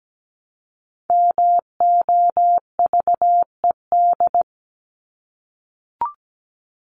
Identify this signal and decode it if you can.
Morse code, heard as a keyed tone, about 17 wpm, 700 Hz. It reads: MOVED
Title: ED Suffix - Top 100 Words with Base Words 17wpm